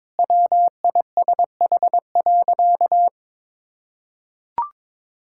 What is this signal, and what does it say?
Morse code, heard as a keyed tone, about 22 wpm, 700 Hz. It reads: WISH.
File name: practice / 22wpm / Sentences with 2-Word Phrases - 22wpm